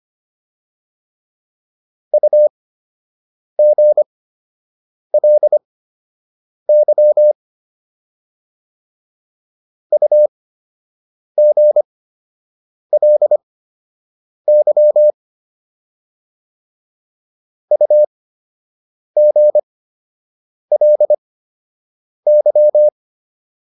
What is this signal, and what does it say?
Morse code, heard as a keyed tone, about 25 wpm, spaced as for 7 wpm, 600 Hz. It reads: UGLY UGLY UGLY
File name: LY4 07